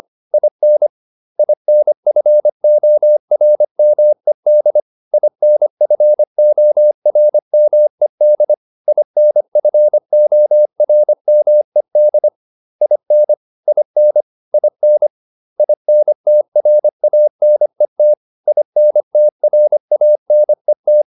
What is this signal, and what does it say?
Morse code, heard as a keyed tone, about 25 wpm, 600 Hz. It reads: IN INFORMED INFORMED INFORMED IN IN IN INTRANET INTRANET